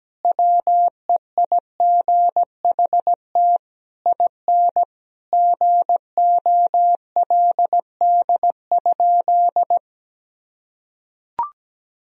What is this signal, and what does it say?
Morse code, heard as a keyed tone, about 17 wpm, 700 Hz. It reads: WEIGHT IN GOLD?
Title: Sentences from Top 1000 Words - Review Part 3 - 17wpm